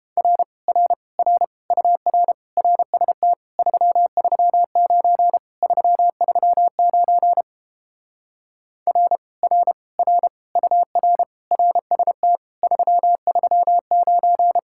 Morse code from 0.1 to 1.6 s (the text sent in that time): R R R